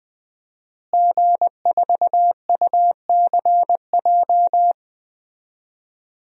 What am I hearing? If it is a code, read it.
Morse code, heard as a keyed tone, about 20 wpm, 700 Hz. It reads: G4UCJ
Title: DXCC-100 Call Signs - 5 Characters - Repeated 2x - 20wpm